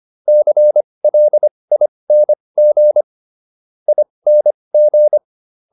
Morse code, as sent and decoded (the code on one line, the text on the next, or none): -.-. .-.. .. -. --. / .. -. --.
CLING ING